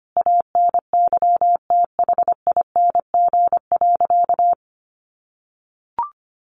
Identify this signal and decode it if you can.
Morse code, heard as a keyed tone, about 25 wpm, 700 Hz. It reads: ANYTHING.
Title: Sentences from Top 1000 Words 25wpm